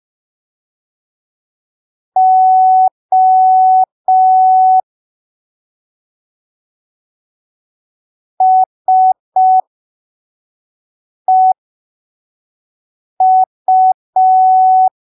Morse code, heard as a keyed tone, about 5 wpm, spaced as for 3 wpm, 750 Hz. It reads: O SEU
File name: arquivos / texto / texto 29 05ppm